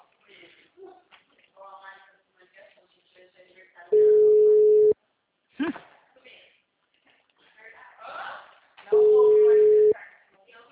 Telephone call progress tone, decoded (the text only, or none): ringback tone